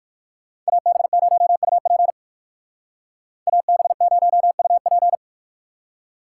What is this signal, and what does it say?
Morse code, heard as a keyed tone, about 59 wpm, 700 Hz. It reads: AB0UP AB0UP